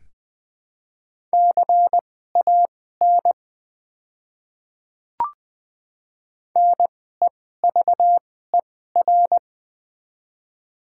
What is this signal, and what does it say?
Morse code, heard as a keyed tone, about 20 wpm, spaced as for 14 wpm, 700 Hz. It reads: CAN NEVER
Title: 2x Character Spacing - Top 300 Words - 20wpm